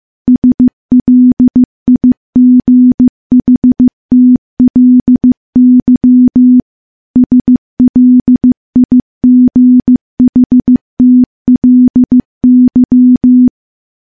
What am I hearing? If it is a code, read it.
Morse code, heard as a keyed tone, about 15 wpm, 250 Hz. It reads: SLIGHTLY SLIGHTLY
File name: morse / morse-training-15WPM